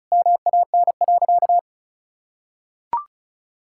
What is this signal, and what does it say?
Morse code, heard as a keyed tone, about 34 wpm, 700 Hz. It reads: MAN.
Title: Sentences from Top 1600 Words 35wpm